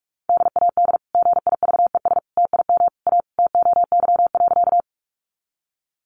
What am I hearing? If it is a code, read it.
Morse code, heard as a keyed tone, about 44 wpm, 700 Hz. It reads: DAD GIVES TIM A TOY.
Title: Easy Phrases 45wpm